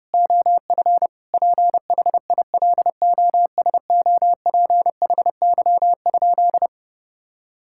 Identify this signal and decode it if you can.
Morse code, heard as a keyed tone, about 32 wpm, 700 Hz. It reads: OF PHILOSOPHY?
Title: Sentences from Top 1000 Words - Review Part 6 - 30wpm